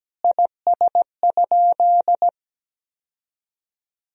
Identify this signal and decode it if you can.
Morse code, heard as a keyed tone, about 17 wpm, 700 Hz. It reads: IS?
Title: Sentences from Top 1000 Words - Review Part 1 - 17wpm